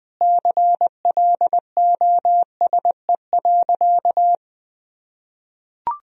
Morse code, heard as a keyed tone, about 21 wpm, 700 Hz. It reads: CLOSE.
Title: Sentences from Top 1000 Words - Review Part 1 - 20wpm